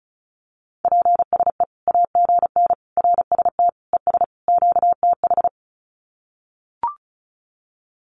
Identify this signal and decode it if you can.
Morse code, heard as a keyed tone, about 34 wpm, 700 Hz. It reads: PSE AGN RST ES QTH
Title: Common QSO Phrases 35wpm